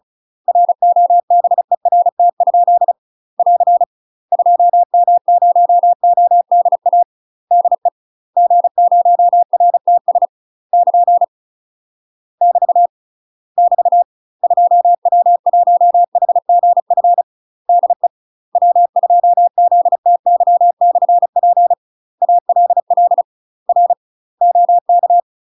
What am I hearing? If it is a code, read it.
Morse code, heard as a keyed tone, about 36 wpm, 700 Hz. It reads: ROBERT? <AR> 2M0ODA DE G0RTS <KN> <BT> <BT> 2W1HGF DE W2ZTY/P ALL R OK
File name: UK MASTER QSOs TRAINING / 35WPM / UK MASTER QSOs 0002